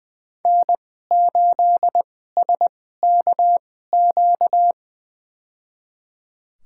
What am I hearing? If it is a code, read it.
Morse code, heard as a keyed tone, about 20 wpm, spaced as for 15 wpm, 700 Hz. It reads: N8SKQ
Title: USA Call Signs - 5 Characters - 2x Character Spacing - 20wpm